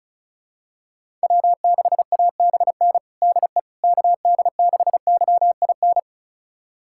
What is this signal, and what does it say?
Morse code, heard as a keyed tone, about 34 wpm, 700 Hz. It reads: W6ABN DE KD6YIN
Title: USA Callsign DE Callsign 35wpm